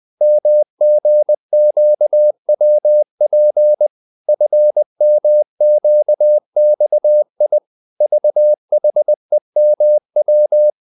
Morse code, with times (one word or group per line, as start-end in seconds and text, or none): 0.2-3.9 MGQWP
4.3-7.6 FMQXI
8.0-10.7 VHEMW